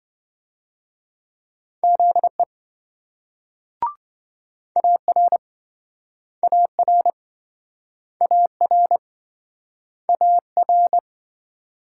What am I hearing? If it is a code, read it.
Morse code, heard as a keyed tone, about 27 wpm, 700 Hz. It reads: ZE AR AR AR AR